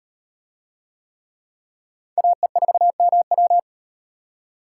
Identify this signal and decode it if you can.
Morse code, heard as a keyed tone, about 38 wpm, 700 Hz. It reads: AE4MW